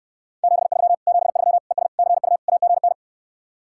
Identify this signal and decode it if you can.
Morse code, heard as a keyed tone, about 68 wpm, 700 Hz. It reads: 73 73 ES CU AGN